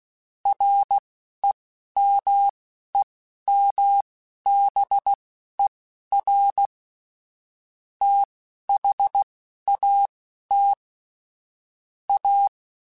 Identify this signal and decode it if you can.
Morse code, heard as a keyed tone, about 16 wpm, spaced as for 11 wpm, 800 Hz. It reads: REMEMBER THAT A
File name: Technical Subjects / Technical Subjects-mp3s / Sw Pwr Sup @10